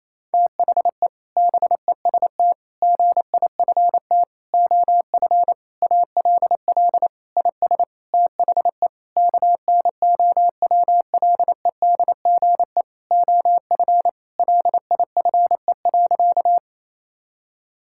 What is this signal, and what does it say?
Morse code, heard as a keyed tone, about 28 wpm, 700 Hz. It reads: THE BEST GIFT OF ALL IS THE KNOWLEDGE OF LIFE.
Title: Sentences from Top 1000 Words - Review Part 6 - 28wpm